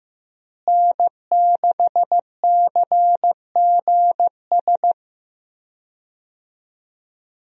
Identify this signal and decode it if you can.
Morse code, heard as a keyed tone, about 15 wpm, 700 Hz. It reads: N6CGS
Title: USA Call Signs - 5 Characters - Repeated 3x - 15wpm